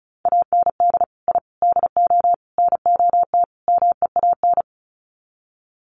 Morse code, heard as a keyed tone, about 34 wpm, 700 Hz. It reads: AND I DO NOT MEAN
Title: Sets of 5 Words Encore2 35wpm